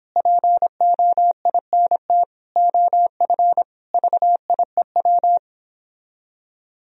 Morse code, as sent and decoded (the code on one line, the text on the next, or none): .--. --- .. -. - / --- ..-. / ...- .. .--
POINT OF VIEW